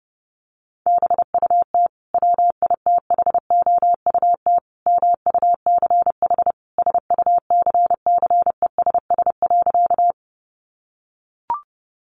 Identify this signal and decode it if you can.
Morse code, heard as a keyed tone, about 30 wpm, 700 Hz. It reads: BUT WITHOUT MUCH SUCCESS.
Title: Sentences from Top 900 Words 30wpm